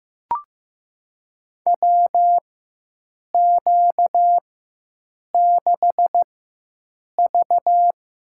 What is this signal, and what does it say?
Morse code, heard as a keyed tone, about 15 wpm, spaced as for 7 wpm, 700 Hz. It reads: WQ6V
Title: USA Call Signs - 4 Characters - 4x Character Spacing - 15wpm